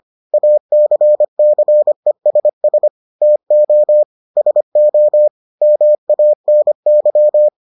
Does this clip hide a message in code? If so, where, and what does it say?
Yes: Morse code from 0.3 to 7.5 s, heard as a keyed tone, about 25 wpm, 600 Hz. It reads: ACCESS TO SO MANY